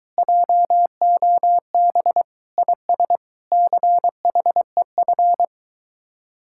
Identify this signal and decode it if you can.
Morse code, heard as a keyed tone, about 22 wpm, 700 Hz. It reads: JOB IS CHEF